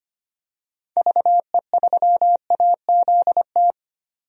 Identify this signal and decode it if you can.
Morse code, heard as a keyed tone, about 25 wpm, 700 Hz. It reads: VE3AZT
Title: DXCC-100 Call Signs - 6 Characters - 25wpm